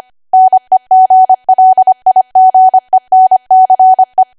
Morse code, heard as a keyed tone, about 25 wpm, 750 Hz. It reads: NEGLIGENCE
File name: cwa / W1AW-Level3 / 25WPM